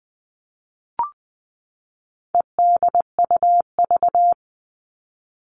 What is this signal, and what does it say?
Morse code, heard as a keyed tone, about 20 wpm, 700 Hz. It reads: EDUV
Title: All Cut Numbers - 4-Digit 20wpm